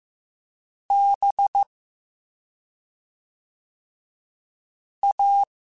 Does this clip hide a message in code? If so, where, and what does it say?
Yes: Morse code from 0.9 to 5.4 s, heard as a keyed tone, about 15 wpm, 800 Hz. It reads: B A